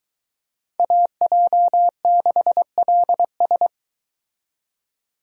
Morse code, heard as a keyed tone, about 23 wpm, 700 Hz. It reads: AJ6LS